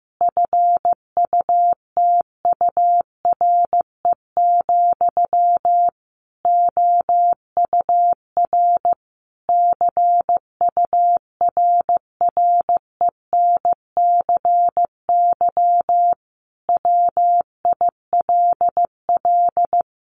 Morse code, heard as a keyed tone, about 15 wpm, 700 Hz. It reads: FUTURE, OUR CURRENCY WILL
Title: Sentences from Top 1500 Words 15wpm